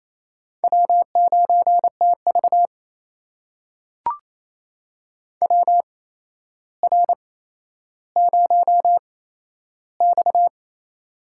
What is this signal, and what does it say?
Morse code, heard as a keyed tone, about 29 wpm, spaced as for 12 wpm, 700 Hz. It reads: W9TV WR0X
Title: USA Call Signs - 4 Characters - 8x Character Spacing - 28wpm